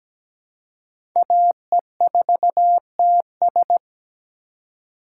Morse code, heard as a keyed tone, about 17 wpm, 700 Hz. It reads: AE4TS
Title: USA Call Signs - 5 Characters - 4x Character Spacing - 17wpm